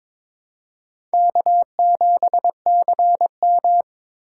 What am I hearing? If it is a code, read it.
Morse code, heard as a keyed tone, about 22 wpm, 700 Hz. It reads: K7CM
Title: USA Call Signs - 4 Characters - 2x Character Spacing - 22wpm